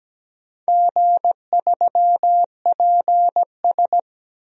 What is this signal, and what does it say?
Morse code, heard as a keyed tone, about 17 wpm, 700 Hz. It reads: G3PS
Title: DXCC-100 Call Signs - 4 Characters - 17wpm